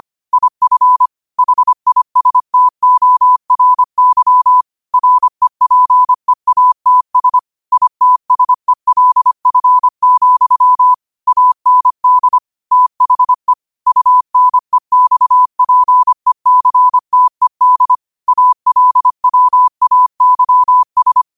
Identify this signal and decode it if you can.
Morse code, heard as a keyed tone, about 25 wpm, 1000 Hz. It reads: IF HISTORY REPEATS ITSELF, AND THE UNEXPECTED ALWAYS